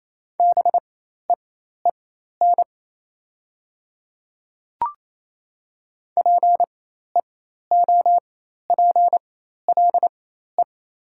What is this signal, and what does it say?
Morse code, heard as a keyed tone, about 28 wpm, spaced as for 13 wpm, 700 Hz. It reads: BEEN PEOPLE
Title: 4x Character Spacing - Top 100 Words - 28wpm